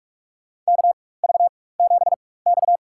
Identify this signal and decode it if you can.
Morse code, heard as a keyed tone, about 44 wpm, spaced as for 21 wpm, 690 Hz. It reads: KV7X
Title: USA Call Signs - 4 Characters - 4x Character Spacing - 45wpm